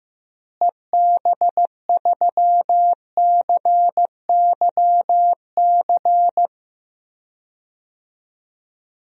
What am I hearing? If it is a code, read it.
Morse code, heard as a keyed tone, about 15 wpm, 700 Hz. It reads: EB3CYC
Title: DXCC-100 Call Signs - 6 Characters - Repeated 3x - 15wpm